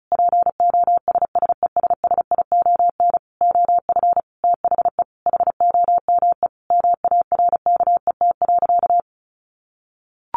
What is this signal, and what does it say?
Morse code, heard as a keyed tone, about 34 wpm, 700 Hz. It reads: POSSESSION OF THE HOME MARKET.
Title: Sentences from Top 1400 Words 35wpm